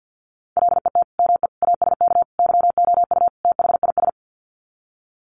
Morse code, heard as a keyed tone, about 50 wpm, 700 Hz. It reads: LET ME ASK YOU THIS
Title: Sets of 5 Words 50wpm